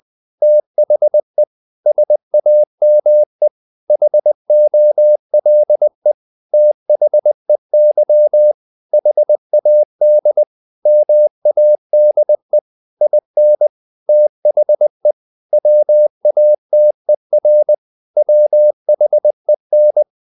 Morse code, heard as a keyed tone, about 20 wpm, 600 Hz. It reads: THE SAME HOLE THEY HAD MADE IN THE WATER WHEN
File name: the old man and the sea part 2 20wpm